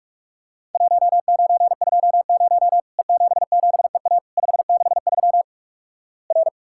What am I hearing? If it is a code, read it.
Morse code, heard as a keyed tone, about 45 wpm, 700 Hz. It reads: 1920 E77EA 563 R
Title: Worked All Europe DX Contest - QTC Exchange - 45wpm